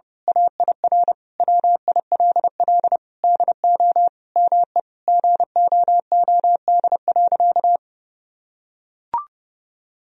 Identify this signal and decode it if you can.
Morse code, heard as a keyed tone, about 30 wpm, 700 Hz. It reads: AIR WILL DO ME GOOD.
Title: Sentences from Top 1000 Words - Review Part 1 - 30wpm